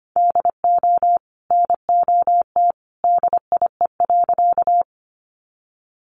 Morse code, heard as a keyed tone, about 25 wpm, 700 Hz. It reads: DO NOT DIE.
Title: Sentences from Top 1300 Words 25wpm